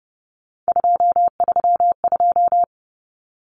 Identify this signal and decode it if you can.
Morse code, heard as a keyed tone, about 30 wpm, 700 Hz. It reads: 232
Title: Common Cut Numbers - 3-Digit 30wpm